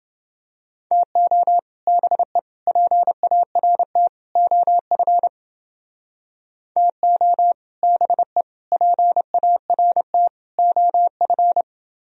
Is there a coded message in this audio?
Yes: Morse code from 0.9 to 11.6 s, heard as a keyed tone, about 30 wpm, 700 Hz. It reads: TO BE PART OF TO BE PART OF